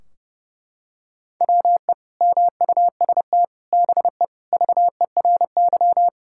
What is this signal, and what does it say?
Morse code, heard as a keyed tone, about 32 wpm, 700 Hz. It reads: WE MUST BE VERY
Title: Sentences from Top 1600 Words 30wpm